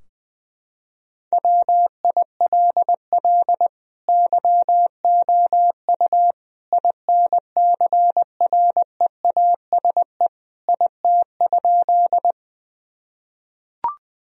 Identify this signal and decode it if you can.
Morse code, heard as a keyed tone, about 20 wpm, 700 Hz. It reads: WILL YOU INCREASE IT?
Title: Sentences from Top 1000 Words - Review Part 4 - 20wpm